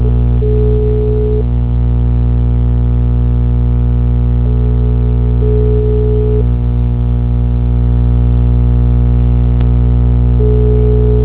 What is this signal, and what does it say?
Ringback tone (Europe, 425 Hz), with noise.
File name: out- -2003- - - .39434